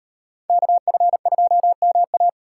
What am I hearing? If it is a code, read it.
Morse code, heard as a keyed tone, about 36 wpm, 700 Hz. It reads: KF2MA